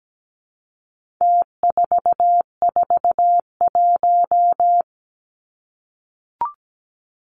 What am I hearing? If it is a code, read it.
Morse code, heard as a keyed tone, about 17 wpm, 700 Hz. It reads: T441
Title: Numbers - Contest Style - 4-Digit 17wpm